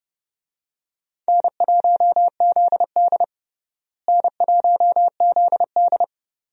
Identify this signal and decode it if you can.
Morse code, heard as a keyed tone, about 30 wpm, 700 Hz. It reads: N1ZD N1ZD
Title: USA Call Signs - 4 Characters - Repeated 2x - 30wpm